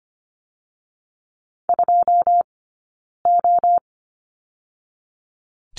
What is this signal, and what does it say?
Morse code, heard as a keyed tone, about 25 wpm, 700 Hz. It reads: 2 O
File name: course / 25:10wpm / 092 Lesson 16d - Letter c - Two Characters 25wpm